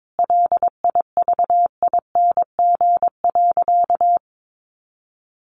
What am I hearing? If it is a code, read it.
Morse code, heard as a keyed tone, about 22 wpm, 700 Hz. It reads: LIVING.
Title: Sentences from Top 1100 Words 22wpm